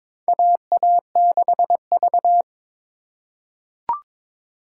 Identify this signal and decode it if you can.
Morse code, heard as a keyed tone, about 21 wpm, 700 Hz. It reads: AA6V
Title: USA Call Signs - 4x Character Spacing - 22wpm